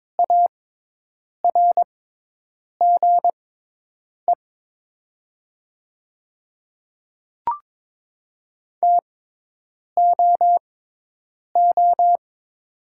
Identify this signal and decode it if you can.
Morse code, heard as a keyed tone, about 22 wpm, spaced as for 8 wpm, 700 Hz. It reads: ARGE TOO
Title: 6x Character Spacing - Top 500 Words - 22wpm